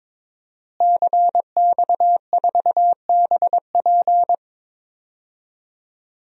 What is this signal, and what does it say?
Morse code, heard as a keyed tone, about 22 wpm, 700 Hz. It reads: CX4BP
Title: DXCC-100 Call Signs - 5 Characters - Repeated 3x - 22wpm